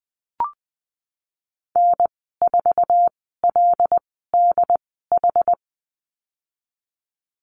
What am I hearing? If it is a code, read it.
Morse code, heard as a keyed tone, about 20 wpm, spaced as for 15 wpm, 700 Hz. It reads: N4LDH
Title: 082 Lesson 14g - Number 2 - Call Signs 20wpm